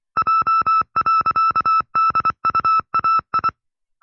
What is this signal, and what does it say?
Morse code, heard as a keyed tone, about 24 wpm, 1310 Hz. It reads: J.DUAI
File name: FreeFiles / CW-practice / 20WPM-side2